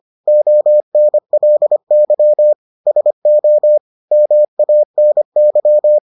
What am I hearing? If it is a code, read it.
Morse code, heard as a keyed tone, about 25 wpm, 600 Hz. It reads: ONLY SO MANY